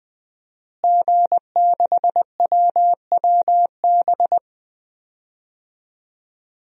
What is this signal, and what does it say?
Morse code, heard as a keyed tone, about 20 wpm, 700 Hz. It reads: G6WWB
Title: DXCC-100 Call Signs - 5 Characters - Repeated 2x - 20wpm